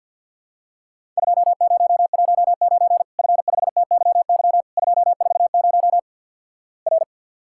Morse code, heard as a keyed tone, about 50 wpm, 700 Hz. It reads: 2019 F5TYY 240 R
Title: Worked All Europe DX Contest - QTC Exchange with Repeats - 50wpm